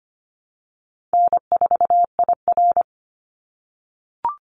Morse code, heard as a keyed tone, about 25 wpm, 700 Hz. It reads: N4IR
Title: Easy Call Signs - 25wpm